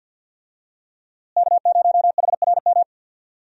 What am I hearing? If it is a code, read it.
Morse code, heard as a keyed tone, about 52 wpm, 700 Hz. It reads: K0HRM
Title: USA Call Signs - 5 Characters - Repeated 2x - 50wpm